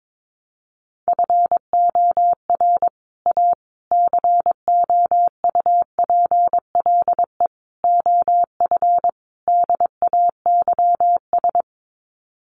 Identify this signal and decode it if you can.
Morse code, heard as a keyed tone, about 22 wpm, 700 Hz. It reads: FOR A COUPLE OF DAYS
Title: Sets of 5 Words Encore2 22wpm